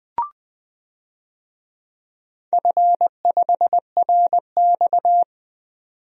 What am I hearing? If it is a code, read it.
Morse code, heard as a keyed tone, about 20 wpm, 700 Hz. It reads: F5RX
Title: DXCC-100 Call Signs - 4 Characters - 20wpm